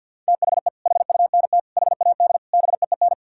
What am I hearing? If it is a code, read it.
Morse code, heard as a keyed tone, about 50 wpm, 700 Hz. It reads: THE HUNT HAD BEEN
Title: Sentences from Top 1600 Words 50wpm